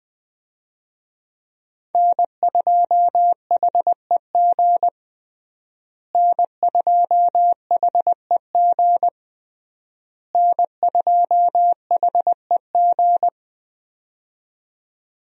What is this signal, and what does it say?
Morse code, heard as a keyed tone, about 20 wpm, 700 Hz. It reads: N2HEG N2HEG N2HEG
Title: USA Call Signs - 5 Characters - Repeated 3x - 20wpm